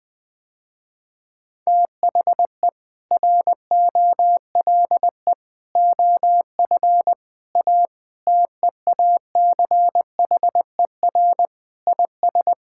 Morse code, heard as a keyed tone, about 20 wpm, 700 Hz. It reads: THE ROLE OF A TEACHER IS
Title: Sentences with 3-Word Phrases - Spoken Beginning - 20wpm